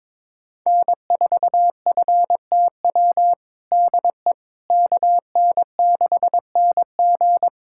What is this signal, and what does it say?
Morse code, heard as a keyed tone, about 21 wpm, 700 Hz. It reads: N4FTW DE KN6NG
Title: USA Callsign DE Callsign - 5 Characters - 22wpm